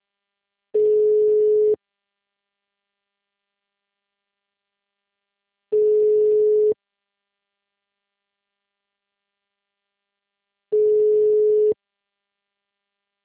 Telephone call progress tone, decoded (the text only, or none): ringback tone